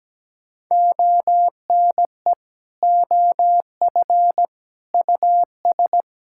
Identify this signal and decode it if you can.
Morse code, heard as a keyed tone, about 17 wpm, 700 Hz. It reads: ONE OF US
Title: Sentences from Top 1600 Words 17wpm